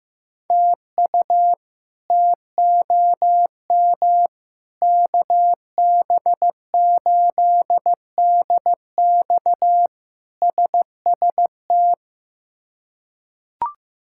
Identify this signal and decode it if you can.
Morse code, heard as a keyed tone, about 15 wpm, 700 Hz. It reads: TU TOM KB8DX SST
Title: K1USN SST QSO Practice - 15wpm